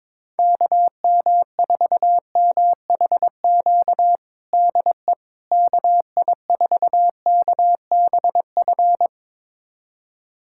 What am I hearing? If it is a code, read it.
Morse code, heard as a keyed tone, about 22 wpm, 700 Hz. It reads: KM4MHQ DE KI4KBF
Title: USA Callsign DE Callsign - 6 Characters - 22wpm